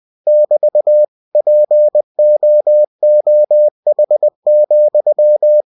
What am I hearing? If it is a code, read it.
Morse code, heard as a keyed tone, about 20 wpm, 600 Hz. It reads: <BT> POOH,
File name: cw-books / winnie-the-pooh / winnie-20wpm-page-0037